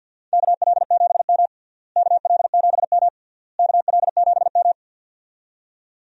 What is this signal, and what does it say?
Morse code, heard as a keyed tone, about 50 wpm, 700 Hz. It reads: KL7M KL7M KL7M
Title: DXCC-100 Call Signs - 4 Characters - Repeated 3x - 50wpm